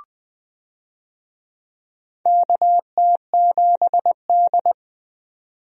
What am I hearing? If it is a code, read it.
Morse code, heard as a keyed tone, about 20 wpm, 700 Hz. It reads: KT7D
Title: USA Call Signs - 4 Characters - 20wpm